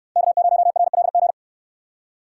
Morse code, heard as a keyed tone, about 65 wpm, 700 Hz. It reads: K0RLG